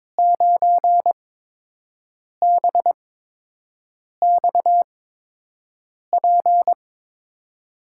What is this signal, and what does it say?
Morse code, heard as a keyed tone, about 22 wpm, spaced as for 6 wpm, 700 Hz. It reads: 9BXP
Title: USA Call Signs - 8x Character Spacing - 22wpm